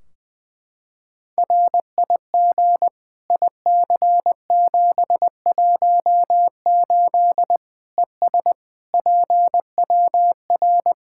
Morse code, heard as a keyed tone, about 21 wpm, 700 Hz. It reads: RIG IC718 ES PWR
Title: Practice QSO - Exchange 2 - Info - 20wpm